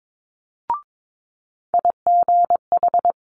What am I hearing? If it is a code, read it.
Morse code, heard as a keyed tone, about 22 wpm, 700 Hz. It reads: IGH
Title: Multi-letter Phonemes Quick Fire 22wpm